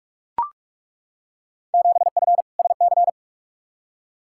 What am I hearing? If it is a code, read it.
Morse code, heard as a keyed tone, about 43 wpm, 700 Hz. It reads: 7F SC
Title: Field Day Exchanges Encore - 45wpm